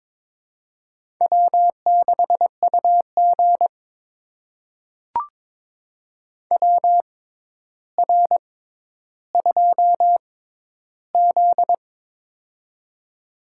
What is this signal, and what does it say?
Morse code, heard as a keyed tone, about 22 wpm, spaced as for 11 wpm, 700 Hz. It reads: W6UG WR2Z
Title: USA Call Signs - 4 Characters - 6x Character Spacing - 22wpm